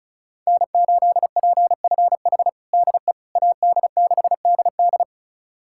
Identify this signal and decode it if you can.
Morse code, heard as a keyed tone, about 36 wpm, 700 Hz. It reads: N8PFH DE AD6DD
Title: USA Callsign DE Callsign - 5 Characters - 35wpm